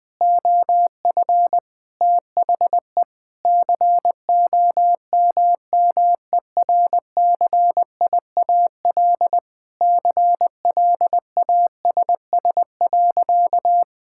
Morse code, heard as a keyed tone, about 20 wpm, 700 Hz. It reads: OF THE COMMERCIAL CLASS.
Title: Sentences from Top 1000 Words - Review Part 5 - 20wpm